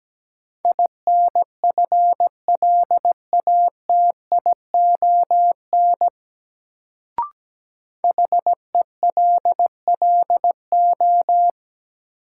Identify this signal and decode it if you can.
Morse code, heard as a keyed tone, about 17 wpm, 700 Hz. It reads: INFLATION HELLO
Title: Sentences from Top 1100 Words - New Word Drill - 17wpm